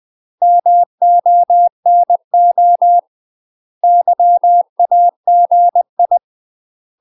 Morse code, heard as a keyed tone, about 20 wpm, 700 Hz. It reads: MONO YAGI
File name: UK MASTER QSOs TRAINING / 20WPM / UK MASTER QSOs 0058